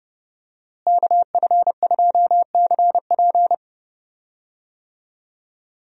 Morse code, heard as a keyed tone, about 32 wpm, 700 Hz. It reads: KF2CP
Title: USA Call Signs - 5 Characters - Repeated 3x - 30wpm